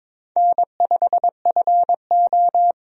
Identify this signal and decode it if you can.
Morse code, heard as a keyed tone, about 22 wpm, 700 Hz. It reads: N5FO
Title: USA Call Signs - 4 Characters - 22wpm